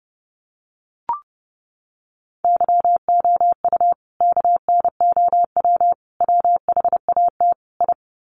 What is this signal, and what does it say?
Morse code, heard as a keyed tone, about 30 wpm, 700 Hz. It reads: YOU KNOW WHAT I
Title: Sets of 4 Words 30wpm